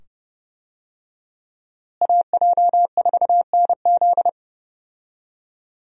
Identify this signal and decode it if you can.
Morse code, heard as a keyed tone, about 30 wpm, 700 Hz. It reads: AJ4NZ
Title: USA Call Signs - 5 Characters - Repeated 2x - 30wpm